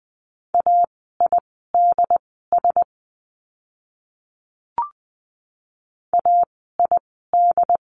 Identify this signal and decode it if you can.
Morse code, heard as a keyed tone, about 20 wpm, spaced as for 14 wpm, 700 Hz. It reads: AIDS AID
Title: 068 Lesson 12f - Letter d - Words 20wpm